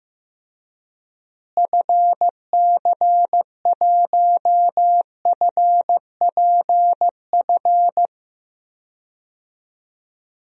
Morse code, heard as a keyed tone, about 15 wpm, 700 Hz. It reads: FC1FPF
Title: DXCC-100 Call Signs - 6 Characters - Repeated 3x - 15wpm